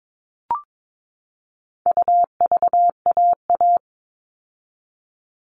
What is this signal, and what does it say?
Morse code, heard as a keyed tone, about 22 wpm, 700 Hz. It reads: UVAA
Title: All Cut Numbers - 4-Digit 22wpm